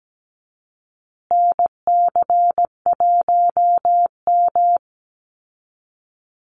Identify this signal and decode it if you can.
Morse code, heard as a keyed tone, about 17 wpm, 700 Hz. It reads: NC1M
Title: Call Signs Repeated 2x - 17wpm